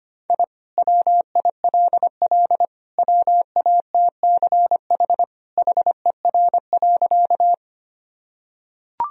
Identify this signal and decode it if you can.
Morse code, heard as a keyed tone, about 25 wpm, 700 Hz. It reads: I WILL WATCH HER.
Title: Sentences from Top 1000 Words - Review Part 1 - 25wpm